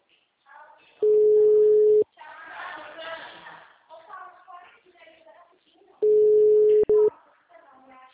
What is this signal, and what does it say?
Ringback tone (Europe, 425 Hz), with speech.